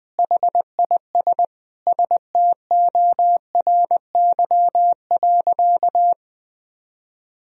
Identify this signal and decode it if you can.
Morse code, heard as a keyed tone, about 20 wpm, 700 Hz. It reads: HIS STORY.
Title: Sentences from Top 1000 Words - Review Part 1 - 20wpm